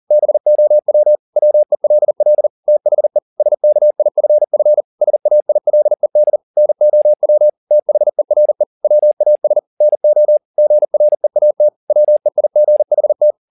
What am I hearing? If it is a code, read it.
Morse code, heard as a keyed tone, about 40 wpm, 600 Hz. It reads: BOW WELL THE SKIFF SAILED NOW THERE WAS NO GREAT WEIGHT
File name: the old man and the sea 2 40wpm